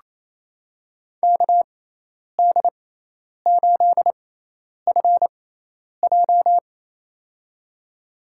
Morse code, heard as a keyed tone, about 28 wpm, spaced as for 10 wpm, 700 Hz. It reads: KD8FJ
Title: USA Call Signs - 5 Characters - 6x Character Spacing - 28wpm